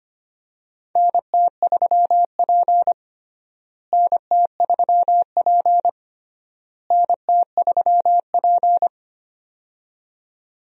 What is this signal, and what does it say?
Morse code, heard as a keyed tone, about 25 wpm, 700 Hz. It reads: NT3P NT3P NT3P
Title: USA Call Signs - 4 Characters - Repeated 3x - 25wpm